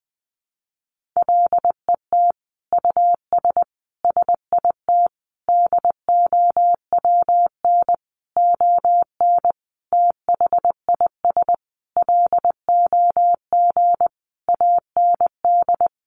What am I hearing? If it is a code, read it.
Morse code, heard as a keyed tone, about 20 wpm, 700 Hz. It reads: LET US SIT DOWN ON THIS LOG AND
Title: Sentences from Top 1400 Words 20wpm